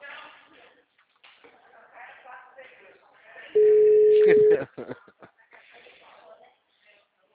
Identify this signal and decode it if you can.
Ringback tone (Europe, 425 Hz), with speech.
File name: out- -2002- - - .113